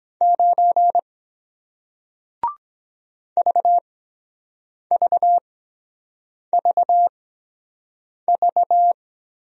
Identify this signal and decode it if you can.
Morse code, heard as a keyed tone, about 23 wpm, spaced as for 7 wpm, 700 Hz. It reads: 9 VVVV